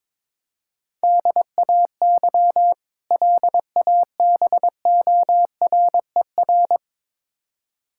Morse code, heard as a keyed tone, about 22 wpm, 700 Hz. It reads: DAY LABORER
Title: K1USN SST QSO Practice 22wpm